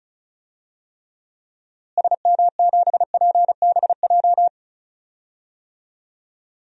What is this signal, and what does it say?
Morse code, heard as a keyed tone, about 36 wpm, 700 Hz. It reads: SM7PBJ
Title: DXCC-100 Call Signs - 6 Characters - Repeated 2x - 35wpm